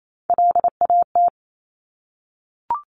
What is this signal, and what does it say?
Morse code, heard as a keyed tone, about 28 wpm, 700 Hz. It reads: LAT
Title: Top Three Letter Combinations in Words 28wpm